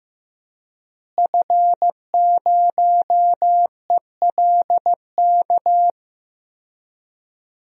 Morse code, heard as a keyed tone, about 15 wpm, 700 Hz. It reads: F0ELK
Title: DXCC-100 Call Signs - 5 Characters - Repeated 2x - 15wpm